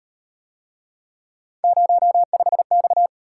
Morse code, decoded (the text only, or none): 05X